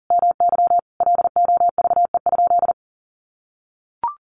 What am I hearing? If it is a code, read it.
Morse code, heard as a keyed tone, about 40 wpm, 700 Hz. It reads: MY LOVE?
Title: Sentences from Top 1500 Words 40wpm